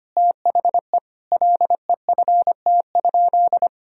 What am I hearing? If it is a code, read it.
Morse code, heard as a keyed tone, about 25 wpm, 700 Hz. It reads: THE LEFT?
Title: Sentences from Top 1000 Words - Review Part 2 - 25wpm